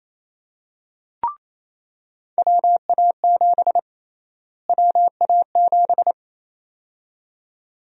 Morse code, heard as a keyed tone, about 27 wpm, 700 Hz. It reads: WA7 WA7
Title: USA Call Sign Prefixes - Repeated 2x - 28wpm